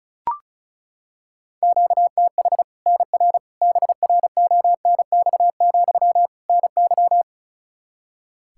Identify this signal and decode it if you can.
Morse code, heard as a keyed tone, about 36 wpm, 700 Hz. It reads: QTH NR BRONX, NY
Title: Practice QSO - Exchange 1 - Beginning - 35wpm